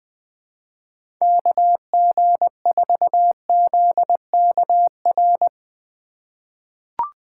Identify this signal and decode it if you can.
Morse code, heard as a keyed tone, about 21 wpm, 700 Hz. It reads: KG4ZKR
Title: USA Call Signs - 6 Characters - 6x Character Spacing - 20wpm